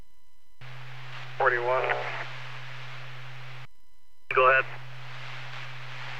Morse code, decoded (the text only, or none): none